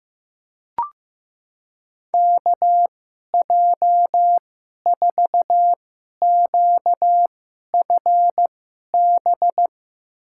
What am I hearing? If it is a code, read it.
Morse code, heard as a keyed tone, about 15 wpm, spaced as for 11 wpm, 700 Hz. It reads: KJ4QFB
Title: USA Call Signs - 2x Character Spacing - 15wpm